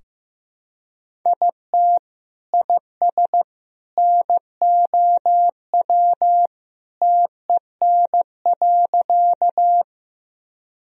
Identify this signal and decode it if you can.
Morse code, heard as a keyed tone, about 16 wpm, 700 Hz. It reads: IT IS NOW TEN.
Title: Sentences from Top 1000 Words - Review Part 2 - 15wpm